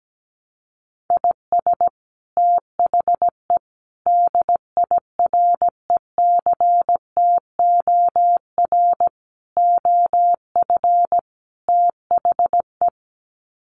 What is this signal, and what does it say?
Morse code, heard as a keyed tone, about 17 wpm, 700 Hz. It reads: IS THE DIRECTOR OF THE
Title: Sets of 5 Words Encore2 17wpm